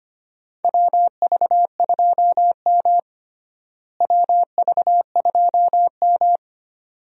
Morse code, heard as a keyed tone, about 25 wpm, 700 Hz. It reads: WV2M WV2M
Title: USA Call Signs - 4 Characters - Repeated 2x - 25wpm